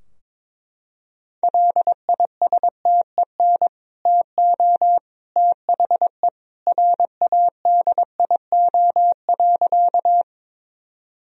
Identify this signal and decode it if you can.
Morse code, heard as a keyed tone, about 22 wpm, 700 Hz. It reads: LISTEN TO THE RADIO.
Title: Sentences from Top 1000 Words - Review Part 3 - 22wpm